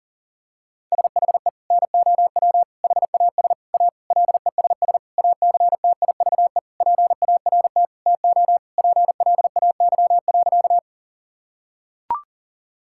Morse code, heard as a keyed tone, about 37 wpm, 700 Hz. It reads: SHE NOW HAS A LESS ACTIVE PART TO PLAY.
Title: Sentences from Top 1000 Words - Review Part 6 - 40wpm